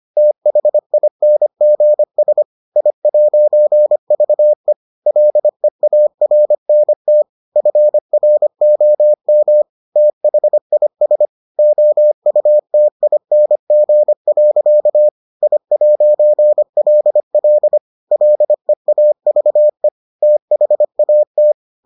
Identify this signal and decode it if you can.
Morse code, heard as a keyed tone, about 25 wpm, 600 Hz. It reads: THINGS I'VE LEARNT FROM THIS OUTING. I'LL LEAVE THAT